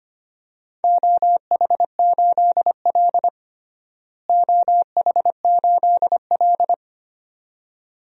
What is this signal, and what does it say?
Morse code, heard as a keyed tone, about 25 wpm, 700 Hz. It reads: OH8L OH8L
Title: DXCC-100 Call Signs - 4 Characters - Repeated 2x - 25wpm